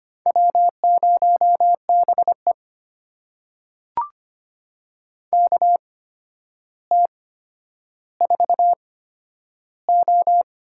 Morse code, heard as a keyed tone, about 25 wpm, spaced as for 11 wpm, 700 Hz. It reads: W0BE KT4O
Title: USA Call Signs - 8x Character Spacing - 25wpm